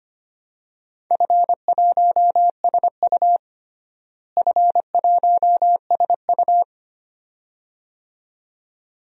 Morse code, heard as a keyed tone, about 25 wpm, 700 Hz. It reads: F1SU F1SU
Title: DXCC-100 Call Signs - 4 Characters - Repeated 2x - 25wpm